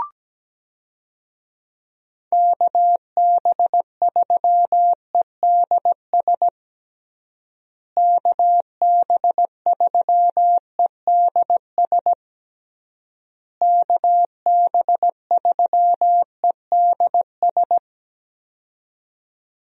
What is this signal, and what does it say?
Morse code, heard as a keyed tone, about 17 wpm, 700 Hz. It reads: KB3EDS KB3EDS KB3EDS
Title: USA Call Signs - 6 Characters - Repeated 3x - 17wpm